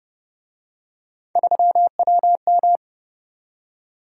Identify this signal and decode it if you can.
Morse code, heard as a keyed tone, about 27 wpm, 700 Hz. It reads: 3WM